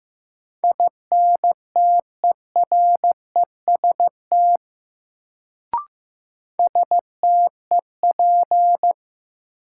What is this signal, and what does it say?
Morse code, heard as a keyed tone, about 15 wpm, 700 Hz. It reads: INTEREST STEP
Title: Sentences from Top 400 Words - New Word Drill - 15wpm